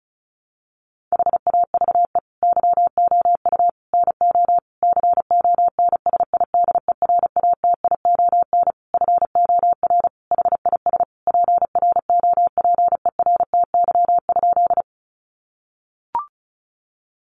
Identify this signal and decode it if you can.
Morse code, heard as a keyed tone, about 34 wpm, 700 Hz. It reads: HAVE YOU NO CONSIDERATION FOR HIS PROPERTY?
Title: Sentences from Top 1500 Words 35wpm